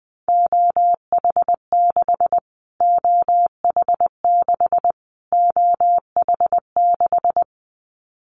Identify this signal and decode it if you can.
Morse code, heard as a keyed tone, about 19 wpm, 700 Hz. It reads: OH6 OH6 OH6
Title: Callsign Prefixes for Top 25 DXCC - Repeated 3x - 20wpm